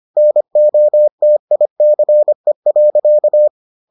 Morse code, heard as a keyed tone, about 25 wpm, 600 Hz. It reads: NOTICE.